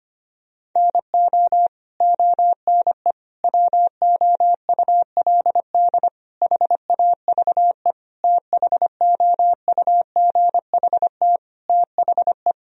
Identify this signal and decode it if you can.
Morse code, heard as a keyed tone, about 25 wpm, 700 Hz. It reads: NO ONE WOULD HAVE THOUGHT THE
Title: Sentences from Top 1000 Words - Review Part 1 - 25wpm